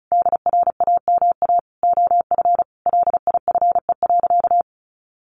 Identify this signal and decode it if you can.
Morse code, heard as a keyed tone, about 34 wpm, 700 Hz. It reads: DRAMA OF LIFE.
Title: Sentences from Top 1500 Words 35wpm